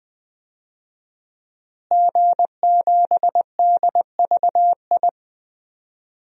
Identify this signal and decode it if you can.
Morse code, heard as a keyed tone, about 21 wpm, 700 Hz. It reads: G7DVI
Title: DXCC-100 Call Signs - 5 Characters - Repeated 3x - 20wpm